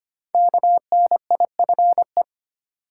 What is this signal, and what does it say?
Morse code, heard as a keyed tone, about 25 wpm, 700 Hz. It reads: KNIFE
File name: 5 Letter Words - First 3 Letters Given - 25wpm